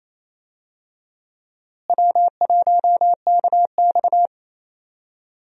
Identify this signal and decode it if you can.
Morse code, heard as a keyed tone, about 28 wpm, 700 Hz. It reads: W1KX
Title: USA Call Signs - 4 Characters - Repeated 2x - 28wpm